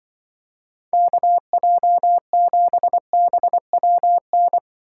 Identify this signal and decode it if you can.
Morse code, heard as a keyed tone, about 24 wpm, 700 Hz. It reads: KJ7BWN
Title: Speed Racing - USA 2x3 Call Signs 20wpm